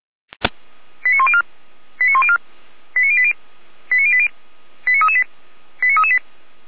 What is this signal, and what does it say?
Five-tone selective call (ZVEI), heard as quick runs of five tones, tones of about 70 ms. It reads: 89104 89104 89080 89080 89338 89338
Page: /455/SDR/ZVEI/